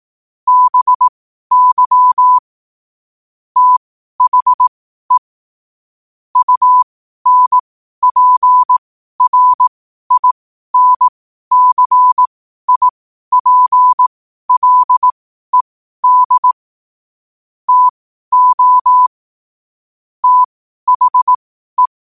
Morse code, heard as a keyed tone, about 18 wpm, spaced as for 12 wpm, 1000 Hz. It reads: BY THE UNPRINCIPLED TO THE
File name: qotdfiles / 12wpm